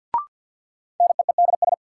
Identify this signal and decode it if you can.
Morse code, heard as a keyed tone, about 53 wpm, 700 Hz. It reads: NEEDS
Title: Sentences from Top 1000 Words - Review Part 3 - New Word Drill - 50wpm